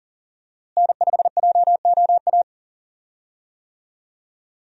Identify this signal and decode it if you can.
Morse code, heard as a keyed tone, about 38 wpm, 700 Hz. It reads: N5JOA
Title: USA Call Signs - 5 Characters - Repeated 3x - 40wpm